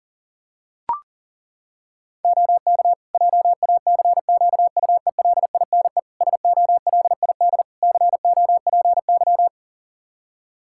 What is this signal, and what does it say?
Morse code, heard as a keyed tone, about 40 wpm, 700 Hz. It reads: OK JACQUELINE SOLID COPY
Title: K1USN SST QSO Practice 40wpm